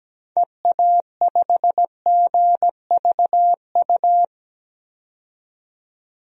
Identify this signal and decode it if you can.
Morse code, heard as a keyed tone, about 17 wpm, 700 Hz. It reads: EA5GVU
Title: DXCC-100 Call Signs - 6 Characters - 17wpm